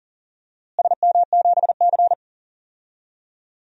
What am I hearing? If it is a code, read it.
Morse code, heard as a keyed tone, about 40 wpm, 700 Hz. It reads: SM7C
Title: DXCC-100 Call Signs - 4 Characters - Repeated 3x - 40wpm